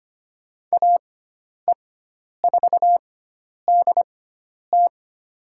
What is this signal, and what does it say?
Morse code, heard as a keyed tone, about 25 wpm, spaced as for 10 wpm, 700 Hz. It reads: AE4DT
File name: Lesson 16g - Letter c - Call Signs 25wpm